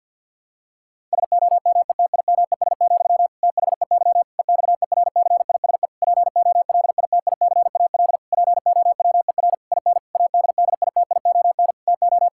Morse code, heard as a keyed tone, about 52 wpm, 700 Hz. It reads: SOMETIMES, THEY EXERCISE POLITICAL POWER IN ADDITION TO
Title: Sentences from Top 1000 Words - Review Part 6 - 50wpm